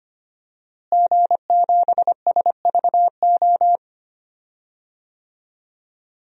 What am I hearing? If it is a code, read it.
Morse code, heard as a keyed tone, about 25 wpm, 700 Hz. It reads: G7SVO
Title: DXCC-100 Call Signs - 5 Characters - Repeated 3x - 25wpm